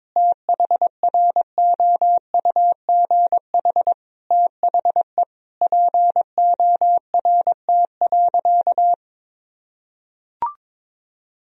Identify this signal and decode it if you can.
Morse code, heard as a keyed tone, about 22 wpm, 700 Hz. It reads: THROUGH THE PORT.
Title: Sentences from Top 1000 Words - Review Part 1 - 22wpm